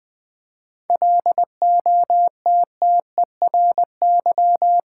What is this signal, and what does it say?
Morse code, heard as a keyed tone, about 20 wpm, 700 Hz. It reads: LOTTERY
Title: 7 Letter Words - First 3 Letters Given - 20wpm